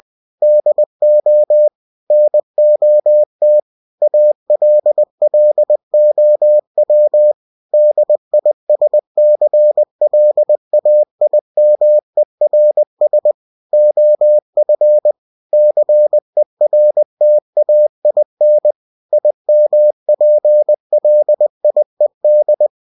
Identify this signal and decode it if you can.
Morse code, heard as a keyed tone, about 20 wpm, 600 Hz. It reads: DO NOT ALLOW DISCLAIMERS OF CERTAIN IMPLIED
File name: cw-books / winnie-the-pooh / winnie-20wpm-page-0119